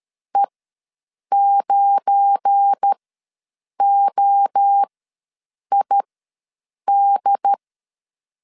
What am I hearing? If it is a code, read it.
Morse code, heard as a keyed tone, about 13 wpm, spaced as for 7 wpm, 800 Hz. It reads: E9OID
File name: LEZIONI / 7 GF9Z / 7 GF9 ws 12wpm 3344